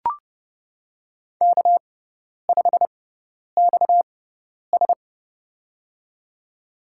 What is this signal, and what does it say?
Morse code, heard as a keyed tone, about 30 wpm, spaced as for 10 wpm, 700 Hz. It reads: K5XS
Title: USA Call Signs - 4 Characters - 6x Character Spacing - 30wpm